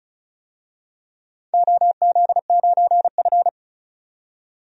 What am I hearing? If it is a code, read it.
Morse code, heard as a keyed tone, about 36 wpm, 700 Hz. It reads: OZ9F
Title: DXCC-100 Call Signs - 4 Characters - 35wpm